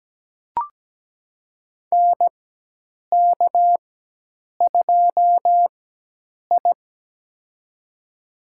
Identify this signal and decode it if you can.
Morse code, heard as a keyed tone, about 17 wpm, spaced as for 8 wpm, 700 Hz. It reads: NK2I
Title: USA Call Signs - 4 Characters - 4x Character Spacing - 17wpm